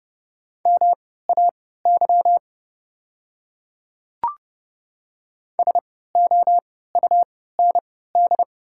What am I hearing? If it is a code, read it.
Morse code, heard as a keyed tone, about 30 wpm, spaced as for 17 wpm, 690 Hz. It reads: MAY SOUND
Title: 3x Character Spacing - Top 100 Words - 30wpm